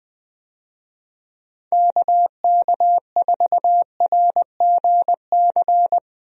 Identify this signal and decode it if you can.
Morse code, heard as a keyed tone, about 21 wpm, 700 Hz. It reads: KK4RGC
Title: USA Call Signs - 6 Characters - Repeated 3x - 20wpm